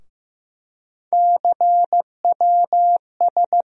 Morse code, heard as a keyed tone, about 15 wpm, 700 Hz. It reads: CWS
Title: USA Call Sign Postfixes - 3 Letters 15wpm